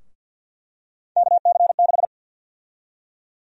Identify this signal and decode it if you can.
Morse code, heard as a keyed tone, about 52 wpm, 700 Hz. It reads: KC6
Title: USA Call Sign Prefixes - Repeated 2x - 50wpm